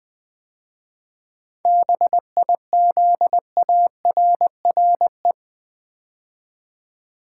Morse code, heard as a keyed tone, about 20 wpm, 700 Hz. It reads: BIZARRE
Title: 7 Letter Words - First 3 Letters Given - 20wpm